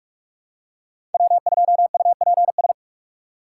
Morse code, heard as a keyed tone, about 41 wpm, 690 Hz. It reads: W2UPS
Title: USA Call Signs - 4x Character Spacing - 45wpm